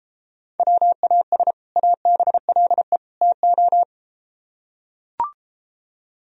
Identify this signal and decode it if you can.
Morse code, heard as a keyed tone, about 33 wpm, 700 Hz. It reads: WAS ABLE TO